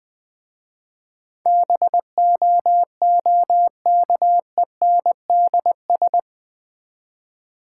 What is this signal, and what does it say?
Morse code, heard as a keyed tone, about 21 wpm, 700 Hz. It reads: BOOKENDS
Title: Easy Words with Hints - 20wpm